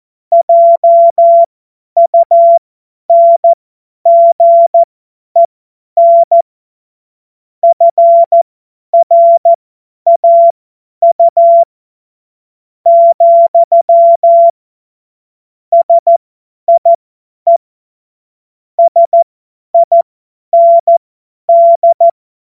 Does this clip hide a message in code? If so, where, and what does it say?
Yes: Morse code from 0.3 to 22.1 s, heard as a keyed tone, about 14 wpm, spaced as for 10 wpm, 680 Hz. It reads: JUNGEN FRAU, SIE SIND